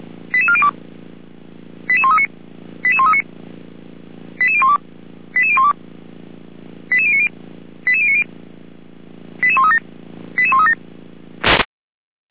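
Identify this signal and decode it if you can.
Five-tone selective call (ZVEI), heard as quick runs of five tones, tones of about 70 ms. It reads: 88402 88139 88139 88012 88012 88090 88090 88137 88137
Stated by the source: ZVEI five-tone selective call 88190 received via SDR.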